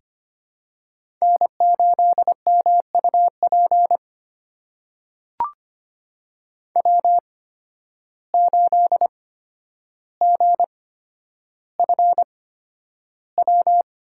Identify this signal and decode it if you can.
Morse code, heard as a keyed tone, about 25 wpm, spaced as for 11 wpm, 700 Hz. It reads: N8MUP W8GFW
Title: USA Call Signs - 5 Characters - 8x Character Spacing - 25wpm